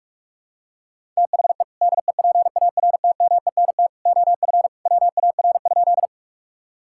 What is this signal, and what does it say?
Morse code, heard as a keyed tone, about 45 wpm, 700 Hz. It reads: THE DEPARTMENT OF WAR?
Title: Sentences from Top 1000 Words - Review Part 5 - 45wpm